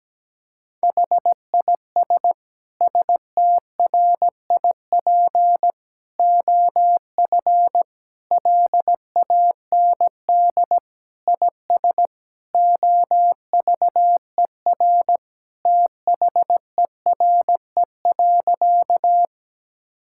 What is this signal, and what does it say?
Morse code, heard as a keyed tone, about 17 wpm, 700 Hz. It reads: HIS STRIP OF LAND IS OVER THERE.
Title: Sentences from Top 1600 Words 17wpm